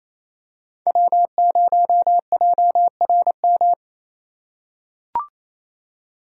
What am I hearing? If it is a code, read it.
Morse code, heard as a keyed tone, about 28 wpm, 700 Hz. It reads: W0JRM
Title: USA Call Signs - 5 Characters - 4x Character Spacing - 28wpm